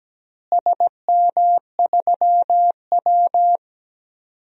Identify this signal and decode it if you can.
Morse code, heard as a keyed tone, about 17 wpm, 700 Hz. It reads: SM3W
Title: DXCC-100 Call Signs - 4 Characters - Repeated 2x - 17wpm